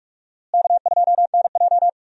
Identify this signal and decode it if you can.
Morse code, heard as a keyed tone, about 46 wpm, 700 Hz. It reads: K2NJ
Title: USA Call Signs - 4 Characters - Repeated 2x - 45wpm